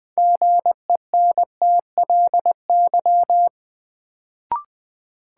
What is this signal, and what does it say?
Morse code, heard as a keyed tone, about 20 wpm, 700 Hz. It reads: GENTLY
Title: Sentences from Top 1500 Words - New Word Drill - 20wpm